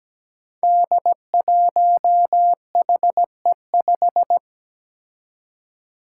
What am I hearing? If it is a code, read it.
Morse code, heard as a keyed tone, about 17 wpm, 700 Hz. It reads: D1HE5
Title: Five Challenging Letter-Number Combinations 17wpm